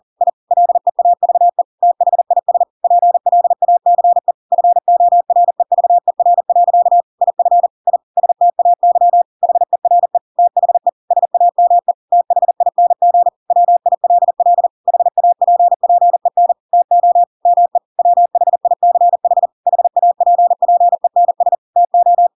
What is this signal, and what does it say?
Morse code, heard as a keyed tone, about 40 wpm, 700 Hz. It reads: I LEAVE THIS PLACE FOREVER. IF I STAY HERE THE SAME THING WILL HAPPEN TO ME WHICH HAPPENS TO
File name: Pinocchio / Pinocchio ch4 40 40 ik1hge